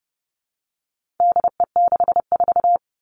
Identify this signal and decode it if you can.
Morse code, heard as a keyed tone, about 30 wpm, 700 Hz. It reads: DE64
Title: All Cut Numbers - 4-Digit 30wpm